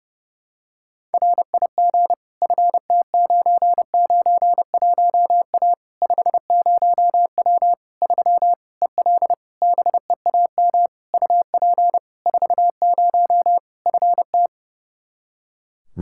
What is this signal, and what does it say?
Morse code, heard as a keyed tone, about 30 wpm, 700 Hz. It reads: RIG FT991A 50W 3 EL BEAM UP 40 FT
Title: Practice QSO - Exchange 2 - Info - 30wpm